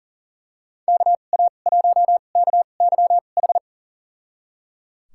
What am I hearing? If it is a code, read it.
Morse code, heard as a keyed tone, about 39 wpm, spaced as for 28 wpm, 700 Hz. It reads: KA1KYH
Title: USA Call Signs - 6 Characters - 2x Character Spacing - 40wpm